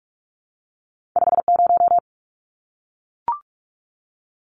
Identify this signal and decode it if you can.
Morse code, heard as a keyed tone, about 46 wpm, 700 Hz. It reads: 50
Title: Numbers - 2-Digit 45wpm